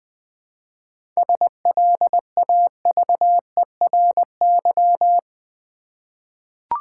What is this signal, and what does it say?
Morse code, heard as a keyed tone, about 20 wpm, 700 Hz. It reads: SLAVERY
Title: 7 Letter Words - First 3 Letters Given - 20wpm